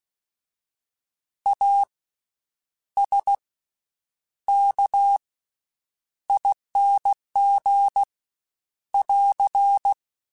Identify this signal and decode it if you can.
Morse code, heard as a keyed tone, about 16 wpm, 800 Hz. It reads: A S K ING <AR>
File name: Special INGs-mp3s / Special INGs-mp3s / Slow ing @15